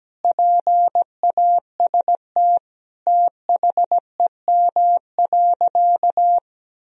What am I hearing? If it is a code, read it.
Morse code, heard as a keyed tone, about 17 wpm, 700 Hz. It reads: PAST THEM.
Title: Sentences from Top 1000 Words - Review Part 3 - 17wpm